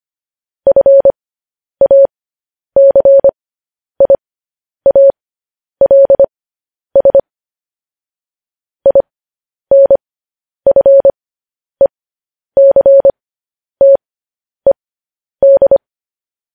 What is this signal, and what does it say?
Morse code, heard as a keyed tone, about 25 wpm, spaced as for 10 wpm, 550 Hz. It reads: FACIALS INFECTED